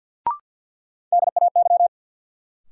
Morse code, heard as a keyed tone, about 53 wpm, 700 Hz. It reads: DAY
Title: Sentences from Top 200 Words - New Word Drill - 50wpm